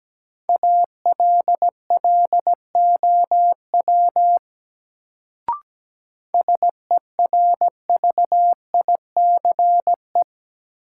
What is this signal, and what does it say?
Morse code, heard as a keyed tone, about 17 wpm, 700 Hz. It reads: ALLOW SERVICE
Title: Sentences from Top 700 Words - New Word Drill - 17wpm